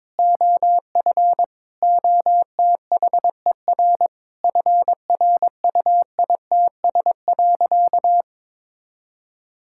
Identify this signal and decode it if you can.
Morse code, heard as a keyed tone, about 22 wpm, 700 Hz. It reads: OF OTHER FRUITS.
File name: Sentences with 7-Word Phrases - 22wpm